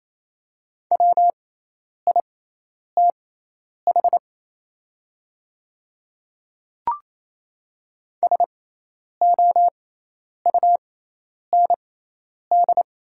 Morse code, heard as a keyed tone, about 27 wpm, spaced as for 10 wpm, 700 Hz. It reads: WITH SOUND
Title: 6x Character Spacing - Top 100 Words - 28wpm